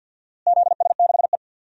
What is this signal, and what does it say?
Morse code, heard as a keyed tone, about 52 wpm, 700 Hz. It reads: ZS6E
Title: DXCC-100 Call Signs - 4 Characters - Repeated 2x - 50wpm